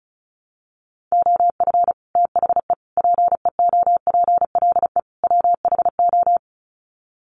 Morse code, heard as a keyed tone, about 34 wpm, 700 Hz. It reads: OF THE PEOPLE WHO
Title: Sets of 4 Words 35wpm